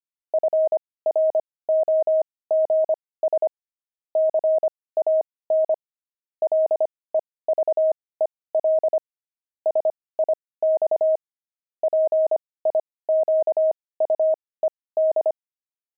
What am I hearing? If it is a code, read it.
Morse code, heard as a keyed tone, about 25 wpm, spaced as for 18 wpm, 630 Hz. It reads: FROGS CAN LEVEL SIX PIQUED
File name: pangram-fs-18-cs-25-frogs